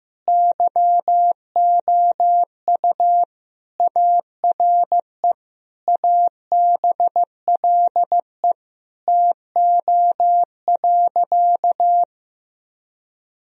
Morse code, heard as a keyed tone, about 15 wpm, 700 Hz. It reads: YOU ARE ABLE TO.
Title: Sentences from Top 1000 Words - Review Part 2 - 15wpm